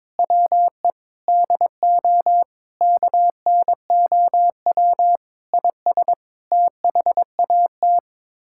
Morse code, heard as a keyed tone, about 22 wpm, 700 Hz. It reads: WE DO KNOW IS THAT
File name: practice / 22wpm / Sets of 6 Words - 22wpm